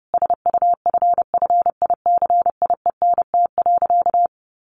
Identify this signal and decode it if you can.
Morse code, heard as a keyed tone, about 30 wpm, 700 Hz. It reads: SUFFICIENT.
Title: Sentences from Top 1500 Words 30wpm